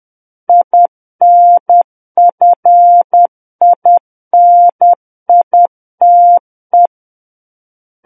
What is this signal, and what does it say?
Morse code, heard as a keyed tone, about 10 wpm, 700 Hz. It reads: INFINITE
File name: Book One / 01-Chapter 1 - 10 WPM